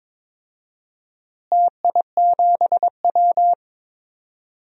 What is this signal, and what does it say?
Morse code, heard as a keyed tone, about 22 wpm, 700 Hz. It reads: TI7W
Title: DXCC-100 Call Signs - 4 Characters - Repeated 3x - 22wpm